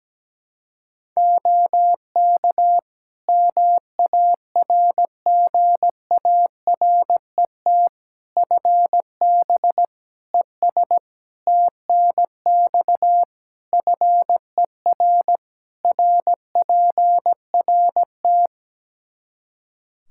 Morse code, heard as a keyed tone, about 17 wpm, 700 Hz. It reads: OK MARGARET FB ES TNX FER RPRT
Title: Practice QSO - Exchange 2 - Info - 17wpm